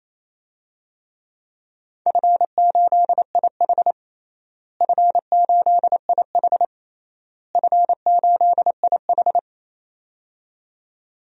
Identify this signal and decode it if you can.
Morse code, heard as a keyed tone, about 28 wpm, 700 Hz. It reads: F8IH F8IH F8IH
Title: DXCC-100 Call Signs - 4 Characters - Repeated 3x - 28wpm